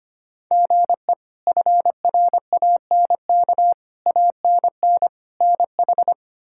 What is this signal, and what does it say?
Morse code, heard as a keyed tone, about 25 wpm, 700 Hz. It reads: GE FRANK ANN NH
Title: K1USN SST QSO Practice - 25wpm